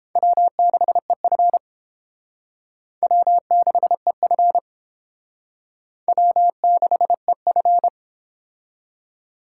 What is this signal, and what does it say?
Morse code, heard as a keyed tone, about 29 wpm, 700 Hz. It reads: W6EF W6EF W6EF